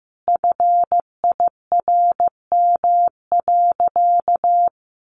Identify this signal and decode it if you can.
Morse code, heard as a keyed tone, about 15 wpm, 700 Hz. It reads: FIRM.
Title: Sentences from Top 1400 Words 15wpm